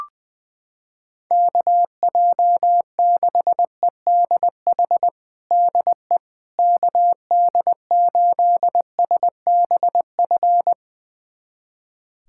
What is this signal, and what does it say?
Morse code, heard as a keyed tone, about 20 wpm, 700 Hz. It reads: KJ6EDH DE KD8SBF
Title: USA Callsign DE Callsign - 6 Characters - 20wpm